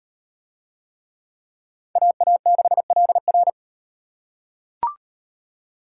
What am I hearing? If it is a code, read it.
Morse code, heard as a keyed tone, about 33 wpm, 700 Hz. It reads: AA6LR